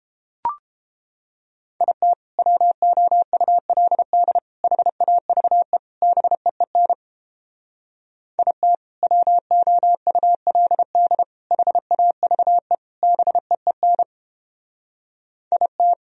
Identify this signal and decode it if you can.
Morse code, heard as a keyed tone, about 31 wpm, 700 Hz. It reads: IT WOULD HAVE BEEN IT WOULD HAVE BEEN IT